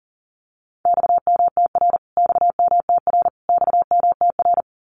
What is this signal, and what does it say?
Morse code, heard as a keyed tone, about 40 wpm, 700 Hz. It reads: XMTR XMTR XMTR
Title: Abbreviations Repeated 3x - 40wpm